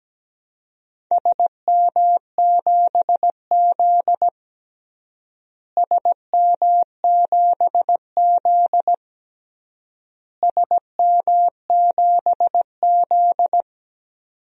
Morse code, heard as a keyed tone, about 17 wpm, 700 Hz. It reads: SM7Z SM7Z SM7Z
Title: DXCC-100 Call Signs - 4 Characters - Repeated 3x - 17wpm